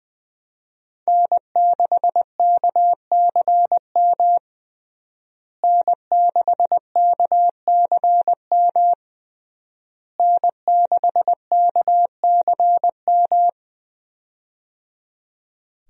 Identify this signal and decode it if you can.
Morse code, heard as a keyed tone, about 20 wpm, 700 Hz. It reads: N6KCM N6KCM N6KCM
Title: USA Call Signs - 5 Characters - Repeated 3x - 20wpm